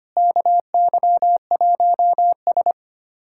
Morse code, heard as a keyed tone, about 25 wpm, 700 Hz. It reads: KY1S
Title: USA Callsign DE Callsign - 4 Characters - 25wpm